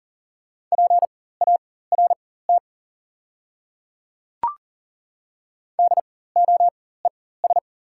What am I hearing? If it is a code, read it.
Morse code, heard as a keyed tone, about 39 wpm, spaced as for 19 wpm, 690 Hz. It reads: PART DOES
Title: 4x Character Spacing - Top 200 Words - 40wpm